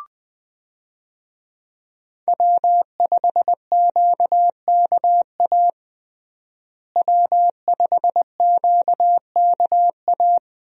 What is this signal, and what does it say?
Morse code, heard as a keyed tone, about 20 wpm, 700 Hz. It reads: W5QKA W5QKA
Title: USA Call Signs - 5 Characters - Repeated 2x - 20wpm